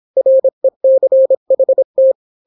Morse code, heard as a keyed tone, about 26 wpm, 530 Hz. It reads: RECHT